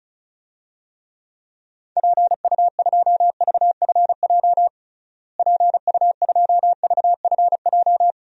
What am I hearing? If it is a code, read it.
Morse code, heard as a keyed tone, about 34 wpm, 700 Hz. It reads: PU2VFJ PU2VFJ
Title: DXCC-100 Call Signs - 6 Characters - Repeated 2x - 35wpm